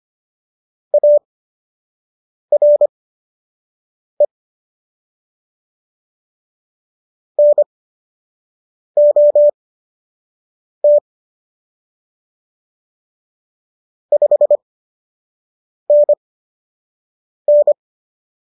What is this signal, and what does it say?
Morse code, heard as a keyed tone, about 25 wpm, spaced as for 6 wpm, 590 Hz. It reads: ARE NOT 5NN